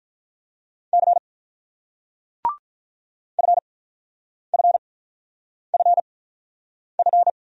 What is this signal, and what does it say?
Morse code, heard as a keyed tone, about 47 wpm, spaced as for 9 wpm, 700 Hz. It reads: C FFFF